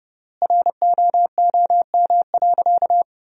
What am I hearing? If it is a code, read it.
Morse code, heard as a keyed tone, about 30 wpm, 700 Hz. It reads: ROOM.
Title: 3x Word-Spacing - Sentences from Top 300 Words - 30wpm